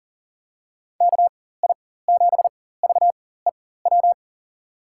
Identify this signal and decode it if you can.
Morse code, heard as a keyed tone, about 42 wpm, spaced as for 19 wpm, 690 Hz. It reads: KI7VEW
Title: USA Call Signs - 4x Character Spacing - 40wpm